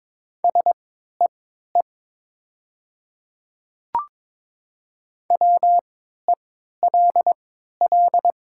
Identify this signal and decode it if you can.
Morse code, heard as a keyed tone, about 22 wpm, spaced as for 12 wpm, 700 Hz. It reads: SEE WELL
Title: 3x Character Spacing - Top 300 Words - 22wpm